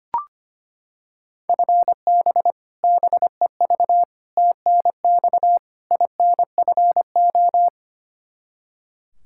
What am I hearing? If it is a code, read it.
Morse code, heard as a keyed tone, about 25 wpm, 700 Hz. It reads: FB BEV TNX INFO
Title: Practice QSO - Exchange 3 - More Info - 25wpm